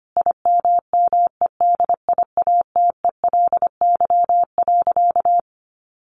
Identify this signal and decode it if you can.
Morse code, heard as a keyed tone, about 25 wpm, 700 Hz. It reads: IMMEDIATELY.
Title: Sentences from Top 1100 Words 25wpm